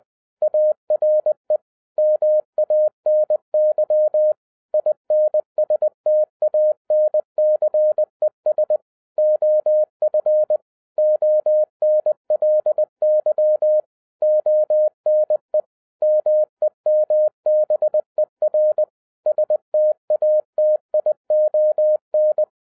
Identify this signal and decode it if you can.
Morse code, heard as a keyed tone, about 20 wpm, 610 Hz. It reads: ARE MANY INSTANCES OF ONLY ONE MEMBER STATION
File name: 2012-01-mp3 / n15-INVESTIGATION INTO THE POSSIBILITY OF WINNING PREFIX VERSION 3 AWARD-20wpm